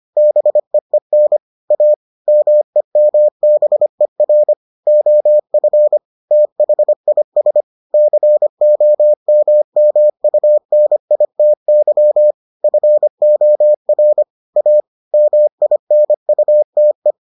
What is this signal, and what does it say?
Morse code, heard as a keyed tone, about 25 wpm, 600 Hz. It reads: BEEN A MEMBER OF THIS COMMUNITY FOR A MINUTE